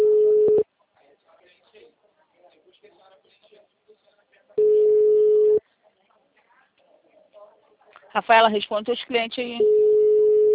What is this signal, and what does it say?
Ringback tone (Europe, 425 Hz), with speech.